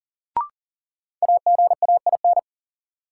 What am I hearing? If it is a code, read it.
Morse code, heard as a keyed tone, about 40 wpm, 700 Hz. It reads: AGAIN
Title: Sets of 4 Words - New Word Drill - 40wpm